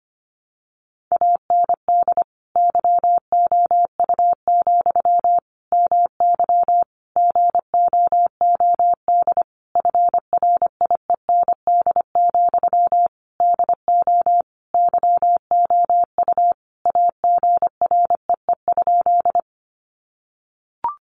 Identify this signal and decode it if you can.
Morse code, heard as a keyed tone, about 25 wpm, 700 Hz. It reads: AND YOU, MY GOOD FRIEND, DO YOU AGREE?
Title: Sentences from Top 1100 Words 25wpm